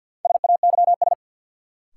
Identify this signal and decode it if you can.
Morse code, heard as a keyed tone, about 53 wpm, 700 Hz. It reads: SAYS
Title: Sets of 2 Words - New Word Drill - 50wpm